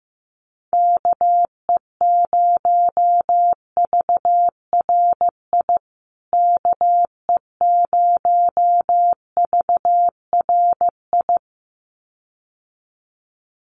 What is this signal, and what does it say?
Morse code, heard as a keyed tone, about 15 wpm, 700 Hz. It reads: KE0VRI KE0VRI
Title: USA Call Signs Repeated 2x 15wpm